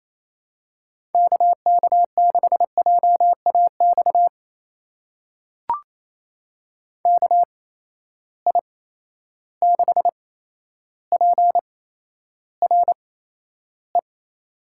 Morse code, heard as a keyed tone, about 28 wpm, spaced as for 12 wpm, 700 Hz. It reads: KK6JAX KI6PRE
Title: USA Call Signs - 6 Characters - 8x Character Spacing - 28wpm